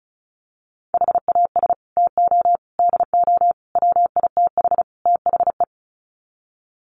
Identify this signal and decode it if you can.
Morse code, heard as a keyed tone, about 34 wpm, 700 Hz. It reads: HAS TO DO WITH THE
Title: Sets of 5 Words 35wpm